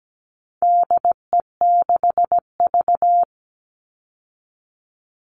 Morse code, heard as a keyed tone, about 17 wpm, 700 Hz. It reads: DE6V
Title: All Cut Numbers - 4-Digit 17wpm